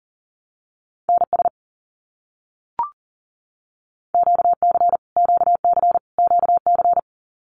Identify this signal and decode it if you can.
Morse code, heard as a keyed tone, about 39 wpm, 700 Hz. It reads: NS QC QC QC
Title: Canadian Provinces and Territories - Repeated 3x - 40wpm